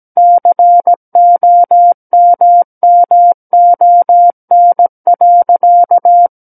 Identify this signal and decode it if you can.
Morse code, heard as a keyed tone, about 17 wpm, 700 Hz. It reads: COMMON.
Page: SKCC Learning Center - Morse Code Practice Files